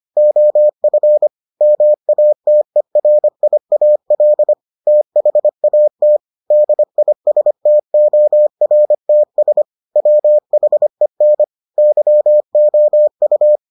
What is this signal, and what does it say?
Morse code, heard as a keyed tone, about 25 wpm, 600 Hz. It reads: OF MATERIAL THAT DISTORTS WHEN YOU